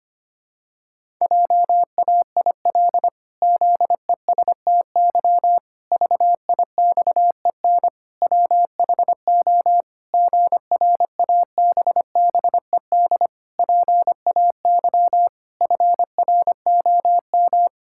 Morse code, heard as a keyed tone, about 25 wpm, 700 Hz. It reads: JAIL ZESTY VIXEN WHO GRABBED PAY FROM
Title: Pangram Challenge v2 - 25wpm